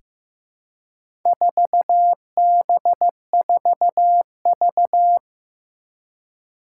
Morse code, heard as a keyed tone, about 15 wpm, 700 Hz. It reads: 4B4V
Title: Four Challenging Letter-Number Combinations 15wpm